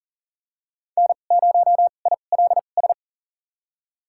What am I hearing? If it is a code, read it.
Morse code, heard as a keyed tone, about 40 wpm, spaced as for 29 wpm, 700 Hz. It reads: N0ILS
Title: USA Call Signs - 5 Characters - 2x Character Spacing - 40wpm